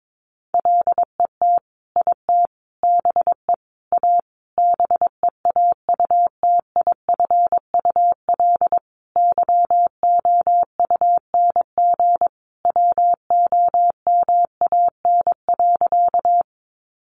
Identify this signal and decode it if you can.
Morse code, heard as a keyed tone, about 22 wpm, 700 Hz. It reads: LET IT BE A BEAUTIFUL YOUNG WOMAN.
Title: Sentences from Top 1100 Words 22wpm